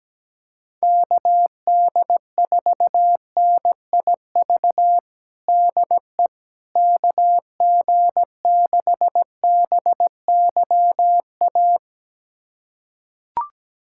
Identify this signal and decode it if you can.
Morse code, heard as a keyed tone, about 17 wpm, 700 Hz. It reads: KD4NIV DE KG6BYA
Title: USA Callsign DE Callsign 17wpm